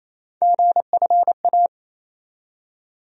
Morse code, heard as a keyed tone, about 27 wpm, 700 Hz. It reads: GFA
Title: USA Call Sign Postfixes - 3 Letters 28wpm